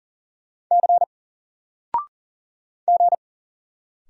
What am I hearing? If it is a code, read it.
Morse code, heard as a keyed tone, about 40 wpm, 700 Hz. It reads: C G
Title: Single Letter-Number 40wpm